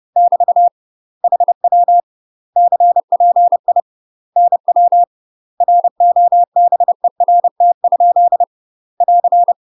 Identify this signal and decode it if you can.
Morse code, heard as a keyed tone, about 30 wpm, 700 Hz. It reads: <BT> HW CPI NW ROBERT? <AR>
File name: UK MASTER QSOs TRAINING / 30WPM / UK MASTER QSOs 0045